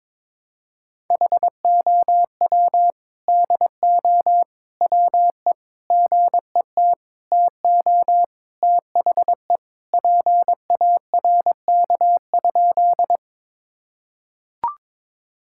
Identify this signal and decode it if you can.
Morse code, heard as a keyed tone, about 22 wpm, 700 Hz. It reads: HOW DO WE GET TO THE PARK?
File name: practice / 22wpm / Sentences with 4-Word Phrases - 22wpm